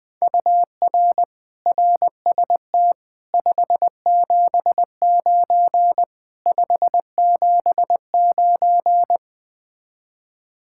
Morse code, heard as a keyed tone, about 20 wpm, 700 Hz. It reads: UR RST 579 579
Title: Speed-Racing Common QSO Phrases - 17wpm